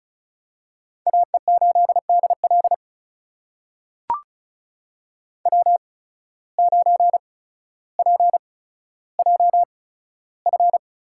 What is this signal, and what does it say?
Morse code, heard as a keyed tone, about 36 wpm, spaced as for 15 wpm, 700 Hz. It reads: AE8DL W9PJF
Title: USA Call Signs - 5 Characters - 8x Character Spacing - 35wpm